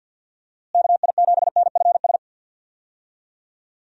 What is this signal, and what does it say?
Morse code, heard as a keyed tone, about 50 wpm, 700 Hz. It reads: KI7NFS
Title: USA Call Signs - 6 Characters - Repeated 2x - 50wpm